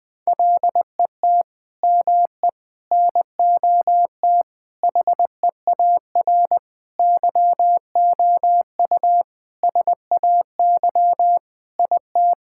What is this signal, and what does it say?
Morse code, heard as a keyed tone, about 20 wpm, 700 Hz. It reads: LET ME NOT HEAR YOU SAY IT
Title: Sentences from Top 1600 Words 20wpm